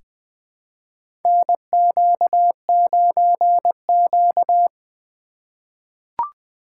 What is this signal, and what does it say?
Morse code, heard as a keyed tone, about 21 wpm, 700 Hz. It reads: NQ9Q
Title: USA Call Signs - 4 Characters - 2x Character Spacing - 20wpm